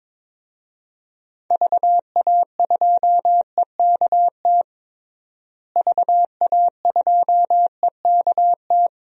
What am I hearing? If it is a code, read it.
Morse code, heard as a keyed tone, about 22 wpm, 700 Hz. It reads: VA2EKT VA2EKT
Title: DXCC-100 Call Signs - 6 Characters - Repeated 2x - 22wpm